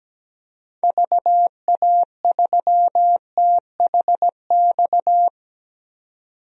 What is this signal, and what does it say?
Morse code, heard as a keyed tone, about 17 wpm, 700 Hz. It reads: VA3THX
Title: DXCC-100 Call Signs - 6 Characters - 17wpm